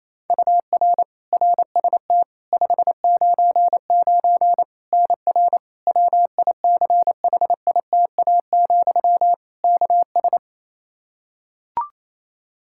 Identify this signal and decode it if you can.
Morse code, heard as a keyed tone, about 28 wpm, 700 Hz. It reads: UR RST 599 NR WICHITA, KS
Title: K1USN SST QSO Practice 28wpm